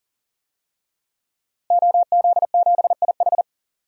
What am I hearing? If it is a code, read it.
Morse code, heard as a keyed tone, about 37 wpm, 690 Hz. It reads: OZ7IH
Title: DXCC-100 Call Signs - 5 Characters - Repeated 2x - 40wpm